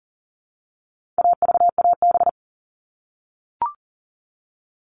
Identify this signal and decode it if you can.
Morse code, heard as a keyed tone, about 40 wpm, 700 Hz. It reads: AVAB
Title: All Cut Numbers - 4-Digit 40wpm